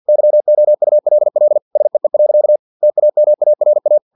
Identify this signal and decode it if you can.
Morse code, heard as a keyed tone, about 50 wpm, 600 Hz. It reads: YOULL SEE. TAMARA
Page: Never 49wpm